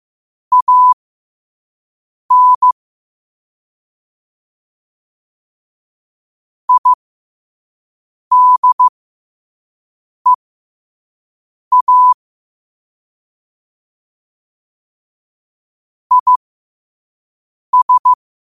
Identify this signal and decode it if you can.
Morse code, heard as a keyed tone, about 15 wpm, spaced as for 5 wpm, 1000 Hz. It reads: AN IDEA IS